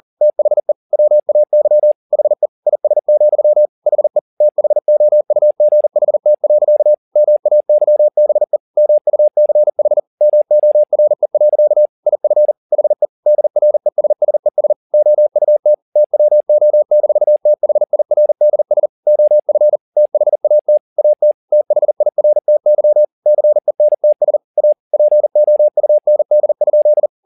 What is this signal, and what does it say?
Morse code, heard as a keyed tone, about 40 wpm, 600 Hz. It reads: THE WAY HE IS, HE THOUGHT. MAYBE MUCH MORE. IF HE DRESSES OUT TWO-THIRDS OF THAT AT THIRTY CENTS A POUND?
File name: the old man and the sea 2 40wpm